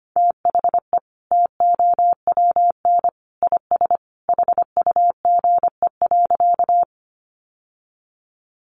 Morse code, heard as a keyed tone, about 25 wpm, 700 Hz. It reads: THE TOWN IS HUGE.
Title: Sentences from Top 1100 Words 25wpm